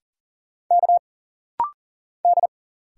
Morse code, heard as a keyed tone, about 40 wpm, 700 Hz. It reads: K D
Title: Single Letter-Number - Rapid-Fire 40wpm